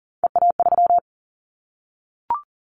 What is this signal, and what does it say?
Morse code, heard as a keyed tone, about 40 wpm, 700 Hz. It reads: EA3
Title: Callsign Prefixes for Top 25 DXCC - 40wpm